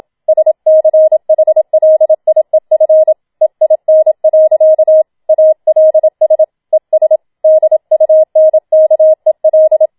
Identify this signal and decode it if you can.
Morse code, heard as a keyed tone, about 27 wpm, 630 Hz. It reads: SCHLIEF EIN. ALS ES DUNKEL
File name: Schneewittchen Teil2 27wpm